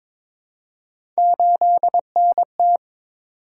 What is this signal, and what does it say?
Morse code, heard as a keyed tone, about 22 wpm, 700 Hz. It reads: 8NT
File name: practice / 22wpm / Three Letter-Number Combinations - 22wpm